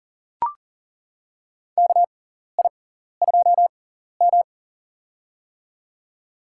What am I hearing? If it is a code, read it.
Morse code, heard as a keyed tone, about 40 wpm, spaced as for 14 wpm, 700 Hz. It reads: KI2M
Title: USA Call Signs - 6x Character Spacing - 40wpm